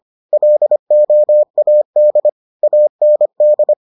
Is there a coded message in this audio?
Yes: Morse code from 0.3 to 3.7 s, heard as a keyed tone, about 25 wpm, 600 Hz. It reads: LOAD AND